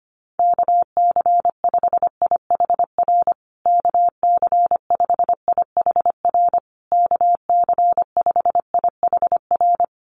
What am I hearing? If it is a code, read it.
Morse code, heard as a keyed tone, about 25 wpm, 700 Hz. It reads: KC5IHR KC5IHR KC5IHR
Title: USA Call Signs Repeated 3x 25wpm